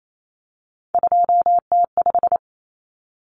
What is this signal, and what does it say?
Morse code, heard as a keyed tone, about 28 wpm, 700 Hz. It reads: 2T5
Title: Common Cut Numbers - 3-Digit 28wpm